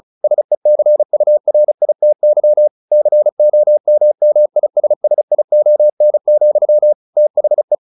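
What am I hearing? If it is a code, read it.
Morse code, heard as a keyed tone, about 34 wpm, 600 Hz. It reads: SECURITY COMMISSION, THE
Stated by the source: Morse practice at 35 wpm from CW med Gnister, sending Ken Follett's novel Never in English.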